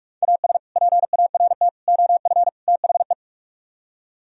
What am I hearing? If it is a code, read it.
Morse code, heard as a keyed tone, about 45 wpm, 700 Hz. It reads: AS PART OF THE